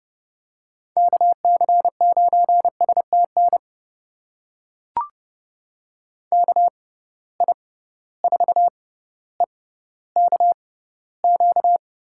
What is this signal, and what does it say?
Morse code, heard as a keyed tone, about 30 wpm, spaced as for 15 wpm, 700 Hz. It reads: KC9STN KI4EKQ
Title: USA Call Signs - 6 Characters - 6x Character Spacing - 30wpm